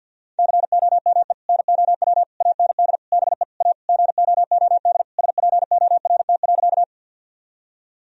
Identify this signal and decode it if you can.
Morse code, heard as a keyed tone, about 50 wpm, 700 Hz. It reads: COME NOW AND BE A GOOD SPORT.
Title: Sentences from Top 1000 Words - Review Part 4 - 50wpm